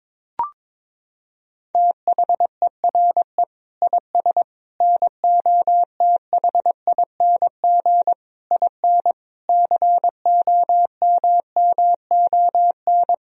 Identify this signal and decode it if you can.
Morse code, heard as a keyed tone, about 22 wpm, 700 Hz. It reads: THERE IS NOTHING IN COMMON
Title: Sentences from Top 1000 Words - Review Part 3 - 22wpm